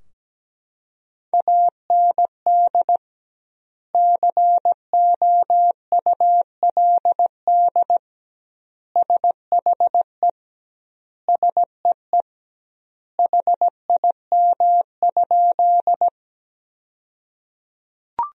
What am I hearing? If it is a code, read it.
Morse code, heard as a keyed tone, about 17 wpm, 700 Hz. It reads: AND COULD SHE SEE HIM?
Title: 2x Word-Spacing - Sentences from Top 100 Words - 17wpm